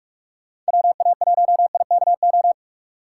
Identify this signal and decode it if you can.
Morse code, heard as a keyed tone, about 53 wpm, 700 Hz. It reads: WA1IKO
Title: Speed Racing - USA 2x3 Call Signs 45wpm